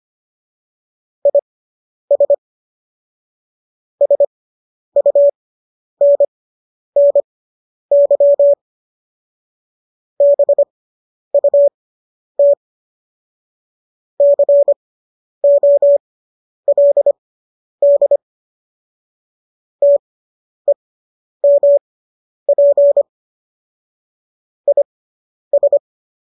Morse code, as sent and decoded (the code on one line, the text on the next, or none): .. ... / ... ..- -. -. -.-- / -... ..- - / -.-. --- .-.. -.. / - . -- .--. / .. ...
IS SUNNY BUT COLD TEMP IS